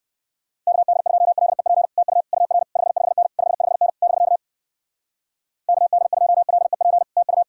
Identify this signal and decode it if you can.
Morse code, heard as a keyed tone, about 56 wpm, 700 Hz. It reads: KD2LEP TU UR 55N 55N <BK> KD2LEP TU
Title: Speed-Racing Common QSO Phrases - 45wpm